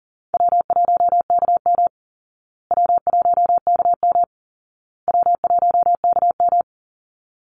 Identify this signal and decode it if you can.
Morse code, heard as a keyed tone, about 40 wpm, 700 Hz. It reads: W1KM W1KM W1KM
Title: Call Signs Repeated 2x - 40wpm